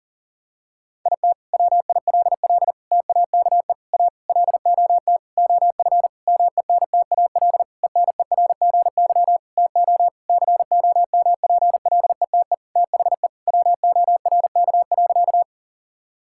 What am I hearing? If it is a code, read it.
Morse code, heard as a keyed tone, about 40 wpm, 700 Hz. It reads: IT WILL TAKE A LOT OF MENTAL ENERGY TO COMPLETE THE WORK.
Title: Sentences from Top 1000 Words - Review Part 6 - 40wpm